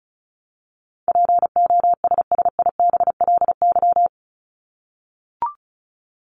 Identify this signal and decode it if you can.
Morse code, heard as a keyed tone, about 34 wpm, 700 Hz. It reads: POSSIBLY
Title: LY Suffix - Top 100 Words with Base Words 35wpm